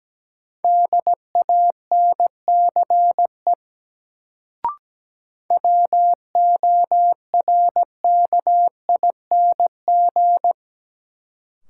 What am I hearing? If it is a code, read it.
Morse code, heard as a keyed tone, about 17 wpm, 700 Hz. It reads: DANCE WORKING
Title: Sentences from Top 700 Words - New Word Drill - 17wpm